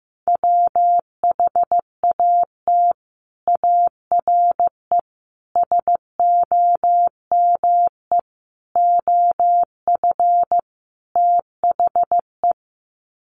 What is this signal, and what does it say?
Morse code, heard as a keyed tone, about 15 wpm, 700 Hz. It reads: WHAT ARE SOME OF THE
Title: Sets of 5 Words Encore 15wpm